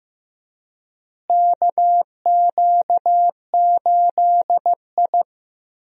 Morse code, heard as a keyed tone, about 15 wpm, 700 Hz. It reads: KQ8I
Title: USA Call Signs - 4 Characters - Repeated 3x - 15wpm